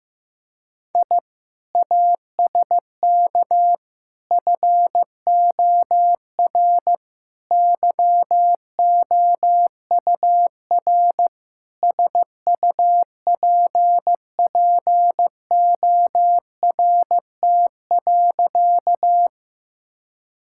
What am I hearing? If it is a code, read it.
Morse code, heard as a keyed tone, about 15 wpm, 700 Hz. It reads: I ASK FOR YOUR SUPPORT.
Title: Sentences from Top 1000 Words - Review Part 5 - 15wpm